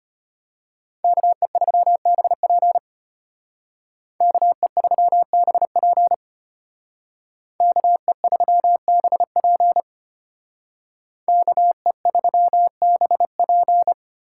Morse code, heard as a keyed tone, about 32 wpm, 700 Hz. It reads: KE3BP KE3BP KE3BP KE3BP